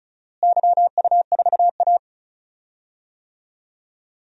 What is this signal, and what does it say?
Morse code, heard as a keyed tone, about 36 wpm, 700 Hz. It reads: YU4A
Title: DXCC-100 Call Signs - 4 Characters - Repeated 3x - 35wpm